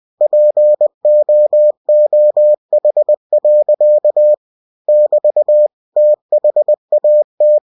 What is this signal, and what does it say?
Morse code, heard as a keyed tone, about 20 wpm, 600 Hz. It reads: POOH. <BT> THAT
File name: cw-books / winnie-the-pooh / winnie-20wpm-page-0027